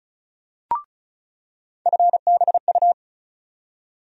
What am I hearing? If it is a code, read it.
Morse code, heard as a keyed tone, about 36 wpm, 700 Hz. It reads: FBU
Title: USA Call Sign Postfixes - 3 Letters 35wpm